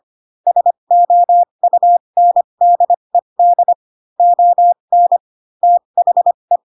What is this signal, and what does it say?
Morse code, heard as a keyed tone, about 25 wpm, spaced as for 22 wpm, 700 Hz. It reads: SOUNDED ON THE
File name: Pinocchio / Pinocchio ch2 18 25 ik1hge